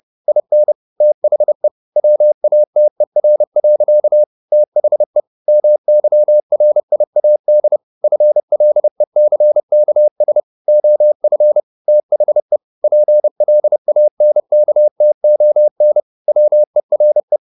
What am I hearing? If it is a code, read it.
Morse code, heard as a keyed tone, about 30 wpm, 600 Hz. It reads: IN THE WATER. THE MYRIAD FLECKS OF THE PLANKTON WERE
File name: the old man and the sea / the old man and the sea 1 30wpm